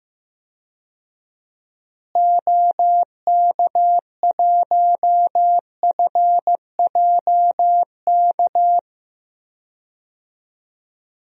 Morse code, heard as a keyed tone, about 15 wpm, 700 Hz. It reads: OK1FJK
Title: DXCC-100 Call Signs - 6 Characters - Repeated 3x - 15wpm